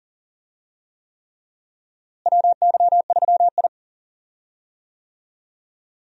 Morse code, heard as a keyed tone, about 40 wpm, 700 Hz. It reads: WY3I
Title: USA Call Signs - 4 Characters - Repeated 3x - 40wpm